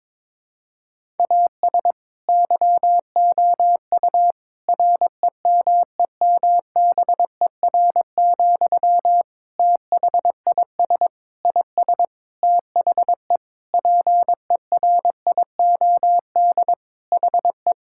Morse code, heard as a keyed tone, about 22 wpm, 700 Hz. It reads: AS YOU REMEMBER, THIS IS THE PERIOD HE
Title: Sentences from Top 1000 Words - Review Part 5 - 22wpm